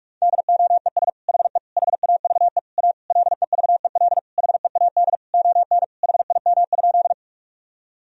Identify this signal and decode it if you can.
Morse code, heard as a keyed tone, about 46 wpm, 700 Hz. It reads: DOES HE HAVE A LEVEL HEAD ON HIM?
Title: Sentences from Top 1000 Words - Review Part 4 - 45wpm